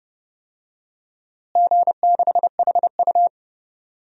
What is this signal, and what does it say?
Morse code, heard as a keyed tone, about 30 wpm, 700 Hz. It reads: G6HU
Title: DXCC-100 Call Signs - 4 Characters - Repeated 2x - 30wpm